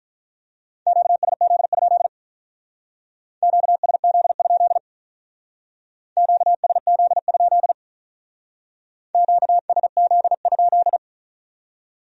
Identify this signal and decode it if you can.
Morse code, heard as a keyed tone, about 43 wpm, 700 Hz. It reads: QSZ? QSZ? QSZ? QSZ?
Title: Speed-Racing Q-Codes - 35wpm